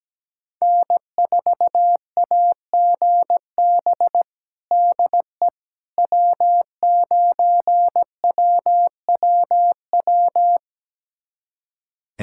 Morse code, heard as a keyed tone, about 17 wpm, 700 Hz. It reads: N4AGB DE W9WWW
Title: USA Callsign DE Callsign - 5 Characters - 17wpm